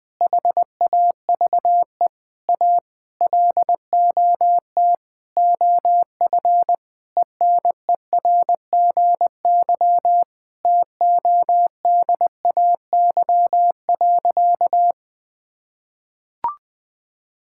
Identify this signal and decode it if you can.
Morse code, heard as a keyed tone, about 21 wpm, 700 Hz. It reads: HAVE A LOT OF ENERGY TODAY.
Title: Sentences from Top 1000 Words - Review Part 6 - 20wpm